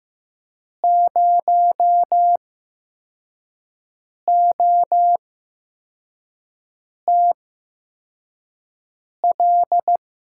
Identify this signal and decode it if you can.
Morse code, heard as a keyed tone, about 15 wpm, spaced as for 4 wpm, 700 Hz. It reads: 0OTL
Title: USA Call Signs - 6 Characters - 8x Character Spacing - 15wpm